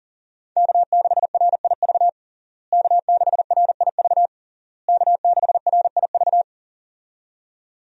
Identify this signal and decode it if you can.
Morse code, heard as a keyed tone, about 37 wpm, 700 Hz. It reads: K6RIV K6RIV K6RIV
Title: USA Call Signs - 5 Characters - Repeated 3x - 40wpm